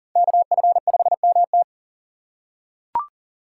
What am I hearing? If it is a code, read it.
Morse code, heard as a keyed tone, about 40 wpm, 700 Hz. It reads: KF5MT
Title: USA Call Signs - 5 Characters - 6x Character Spacing - 40wpm